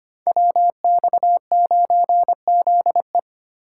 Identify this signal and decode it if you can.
Morse code, heard as a keyed tone, about 25 wpm, 700 Hz. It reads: WX9ZE
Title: Code Groups - 5 Letter:Numbers - 25wpm